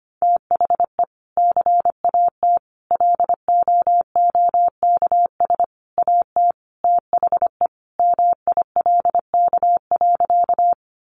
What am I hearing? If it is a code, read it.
Morse code, heard as a keyed tone, about 25 wpm, 700 Hz. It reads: THE CAT LOOKS AT THE MILK.
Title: Additional Easy Phrases 25wpm